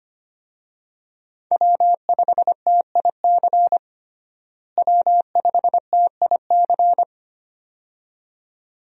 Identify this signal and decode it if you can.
Morse code, heard as a keyed tone, about 25 wpm, 700 Hz. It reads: W5TIC W5TIC
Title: USA Call Signs - 5 Characters - Repeated 2x - 25wpm